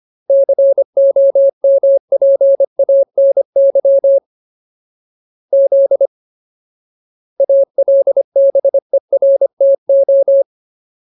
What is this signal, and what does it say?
Morse code, heard as a keyed tone, about 25 wpm, 550 Hz. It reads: COMPANY Z ALBERTO
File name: Ultimate Word List / 25WPM Ultimate